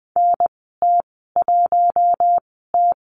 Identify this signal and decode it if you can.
Morse code, heard as a keyed tone, about 20 wpm, spaced as for 15 wpm, 700 Hz. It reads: NT1T
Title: 048 Lesson 9g - Number 4 - Call Signs 20wpm